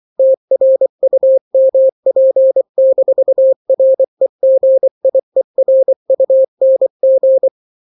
Morse code, heard as a keyed tone, about 24 wpm, 530 Hz. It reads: TRUMP-REGIERUNG
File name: daily / 23wpm